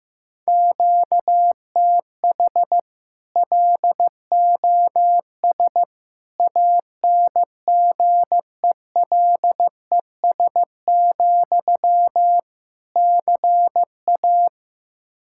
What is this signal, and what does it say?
Morse code, heard as a keyed tone, about 15 wpm, 700 Hz. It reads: QTH LOS ANGELES, CA
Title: Practice QSO - Exchange 1 - Beginning - 15wpm